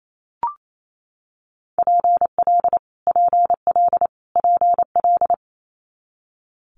Morse code, heard as a keyed tone, about 28 wpm, 700 Hz. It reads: PL PL PL
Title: Multi-letter Phonemes Repeated 3x 28wpm